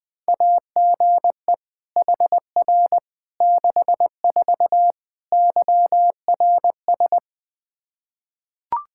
Morse code, heard as a keyed tone, about 20 wpm, 700 Hz. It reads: AGE HR 64 YRS
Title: Practice QSO - Exchange 3 - More Info - 20wpm